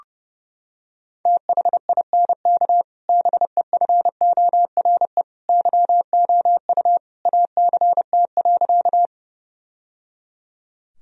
Morse code, heard as a keyed tone, about 30 wpm, 700 Hz. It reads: THINK BEFORE YOU ACT.
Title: Sentences from Top 1000 Words - Review Part 1 - 30wpm